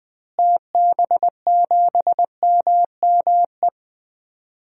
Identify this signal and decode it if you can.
Morse code, heard as a keyed tone, about 20 wpm, 700 Hz. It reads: TB7MME
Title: DXCC-100 Call Signs - 6 Characters - Repeated 3x - 20wpm